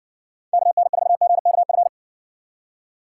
Morse code, heard as a keyed tone, about 55 wpm, 700 Hz. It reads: KN4GKF